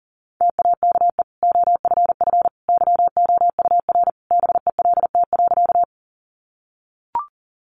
Morse code, heard as a keyed tone, about 37 wpm, 700 Hz. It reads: TAKE OFF YOUR BELT.
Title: Sentences from Top 1400 Words 40wpm